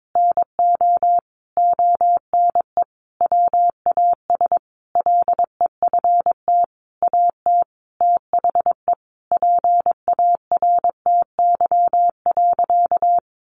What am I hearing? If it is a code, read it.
Morse code, heard as a keyed tone, about 22 wpm, 700 Hz. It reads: NO ONE WAS LEFT AT THE PARTY.
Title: Sentences from Top 1400 Words 22wpm